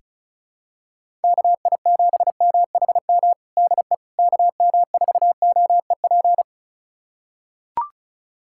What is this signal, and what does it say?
Morse code, heard as a keyed tone, about 36 wpm, 700 Hz. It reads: KI7MHM DE KM4OEP
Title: USA Callsign DE Callsign - 6 Characters - 35wpm